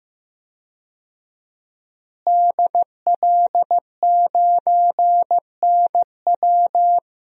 Morse code, heard as a keyed tone, about 15 wpm, 700 Hz. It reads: DL9NW
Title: DXCC-100 Call Signs - 5 Characters - Repeated 2x - 15wpm